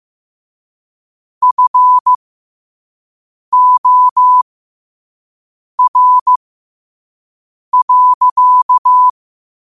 Morse code, heard as a keyed tone, about 15 wpm, spaced as for 5 wpm, 1000 Hz. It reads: FOR.
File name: qotdfiles / 5wpm